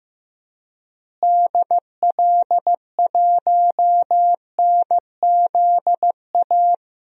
Morse code, heard as a keyed tone, about 15 wpm, 700 Hz. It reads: DL1NZA
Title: DXCC-100 Call Signs - 6 Characters - Repeated 3x - 15wpm